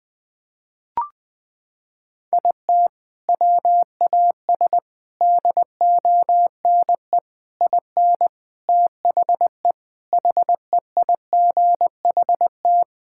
Morse code, heard as a keyed tone, about 20 wpm, 700 Hz. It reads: IT WAS DONE IN THE HEIGHT
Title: Sentences from Top 1600 Words 20wpm